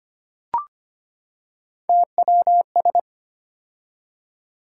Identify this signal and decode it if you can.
Morse code, heard as a keyed tone, about 25 wpm, 700 Hz. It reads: TWS
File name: Three Letter-Number Combinations - 25wpm